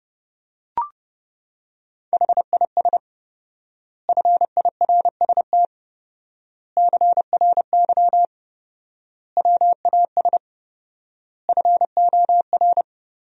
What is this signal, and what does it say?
Morse code, heard as a keyed tone, about 30 wpm, 700 Hz. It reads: HIS FIRST CRY WAS FOR
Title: 4x Word-Spacing - Sentences from Top 500 Words - 30wpm